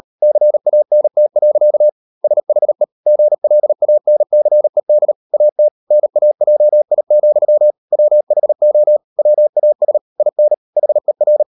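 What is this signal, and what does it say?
Morse code, heard as a keyed tone, about 37 wpm, 600 Hz. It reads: CANT. SHE GLANCED AT NAJI, WHO WAS IN HER
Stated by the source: Morse code practice at 38 WPM, sending Ken Follett's novel Never in English, from CW med Gnister.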